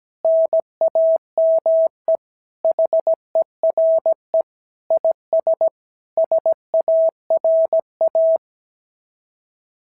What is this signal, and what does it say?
Morse code, heard as a keyed tone, about 17 wpm, 650 Hz. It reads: NAME HERE IS SARA
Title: SOTA QSO Practice 17wpm